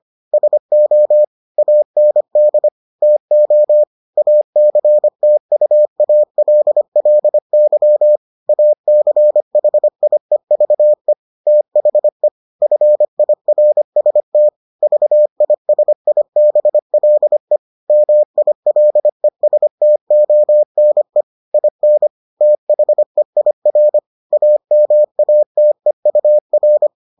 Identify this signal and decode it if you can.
Morse code, heard as a keyed tone, about 25 wpm, 600 Hz. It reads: SO AND TO ACTUALLY ACHIEVE THE FIRST VISIBLE MILESTONE IN THEIR AMATEUR